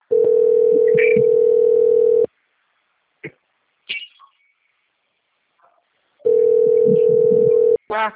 Ringback tone (North America), 440 + 480 Hz.